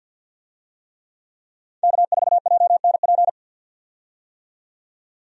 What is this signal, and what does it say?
Morse code, heard as a keyed tone, about 51 wpm, 700 Hz. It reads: K4JNP
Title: USA Call Signs - 5 Characters - 50wpm